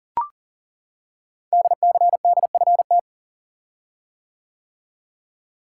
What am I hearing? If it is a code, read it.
Morse code, heard as a keyed tone, about 39 wpm, 700 Hz. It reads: DCDFT
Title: Code Groups - 5 Letters - 40wpm